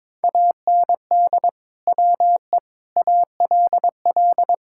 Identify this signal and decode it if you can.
Morse code, heard as a keyed tone, about 22 wpm, 700 Hz. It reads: AND WE ALL
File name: practice / 22wpm / Sentences with 2-Word Phrases - 22wpm